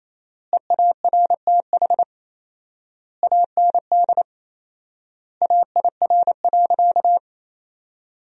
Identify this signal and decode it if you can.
Morse code, heard as a keyed tone, about 28 wpm, 700 Hz. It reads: EARTH AND AIR.
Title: 4x Word-Spacing - Sentences from Top 400 Words - 28wpm